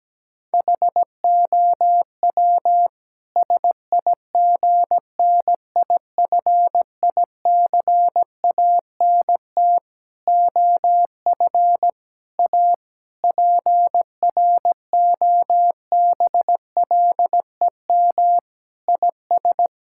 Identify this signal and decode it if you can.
Morse code, heard as a keyed tone, about 17 wpm, 700 Hz. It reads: HOW SIGNIFICANT OF A PROBLEM IS
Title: Sentences from Top 1000 Words - Review Part 6 - 17wpm